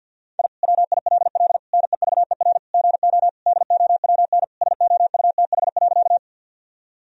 Morse code, heard as a keyed tone, about 48 wpm, 700 Hz. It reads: I WILL NEVER GO DOWN SOUTH.
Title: Sentences from Top 1000 Words - Review Part 2 - 50wpm